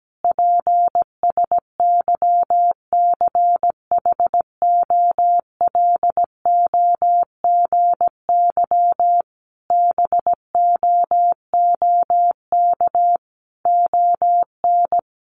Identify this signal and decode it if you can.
Morse code, heard as a keyed tone, about 17 wpm, 700 Hz. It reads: PSYCHOLOGY BOOK ON
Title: Sentences from Top 1400 Words 17wpm